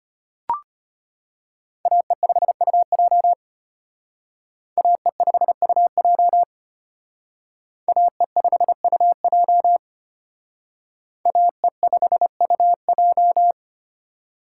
Morse code, heard as a keyed tone, about 32 wpm, 700 Hz. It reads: AE5UJ AE5UJ AE5UJ AE5UJ